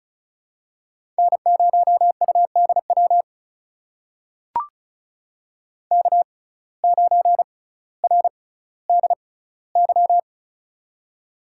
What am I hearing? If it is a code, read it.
Morse code, heard as a keyed tone, about 36 wpm, spaced as for 18 wpm, 700 Hz. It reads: N0UDW K9RDY
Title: USA Call Signs - 5 Characters - 6x Character Spacing - 35wpm